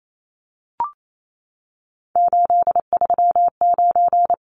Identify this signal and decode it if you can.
Morse code, heard as a keyed tone, about 28 wpm, 700 Hz. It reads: 839
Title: Numbers - 3-Digit 28wpm